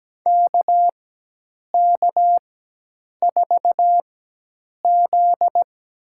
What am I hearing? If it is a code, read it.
Morse code, heard as a keyed tone, about 17 wpm, spaced as for 8 wpm, 700 Hz. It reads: KK4Z
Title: USA Call Signs - 4x Character Spacing - 17wpm